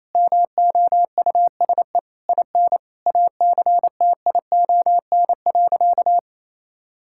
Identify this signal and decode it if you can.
Morse code, heard as a keyed tone, about 28 wpm, 700 Hz. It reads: MOUSE IN ACTION.
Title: Sentences from Top 1000 Words - Review Part 5 - 28wpm